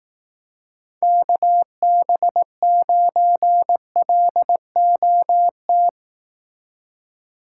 Morse code, heard as a keyed tone, about 18 wpm, 700 Hz. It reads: KB9LOT
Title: Speed Racing - USA Call Signs 15wpm